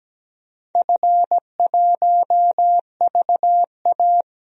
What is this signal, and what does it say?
Morse code, heard as a keyed tone, about 17 wpm, 700 Hz. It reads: F1VA
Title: DXCC-100 Call Signs - 4 Characters - Repeated 3x - 17wpm